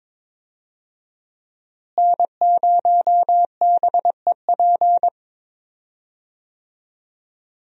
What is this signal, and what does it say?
Morse code, heard as a keyed tone, about 22 wpm, 700 Hz. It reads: N0BEP
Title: USA Call Signs - 5 Characters - Repeated 3x - 22wpm